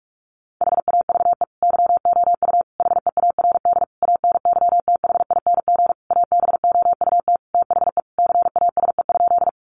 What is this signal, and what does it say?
Morse code, heard as a keyed tone, about 44 wpm, 700 Hz. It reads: HAVE YOU HEARD ANYTHING ABOUT THE CASE?
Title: Sentences from Top 1000 Words 45wpm